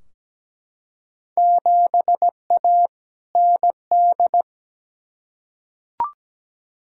Morse code, heard as a keyed tone, about 17 wpm, 700 Hz. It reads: 7A ND
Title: Field Day Exchanges Encore - 17wpm